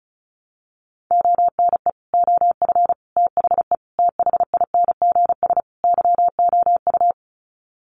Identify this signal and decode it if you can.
Morse code, heard as a keyed tone, about 34 wpm, 700 Hz. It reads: ONE OF THE THINGS YOU
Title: Sets of 5 Words Encore2 35wpm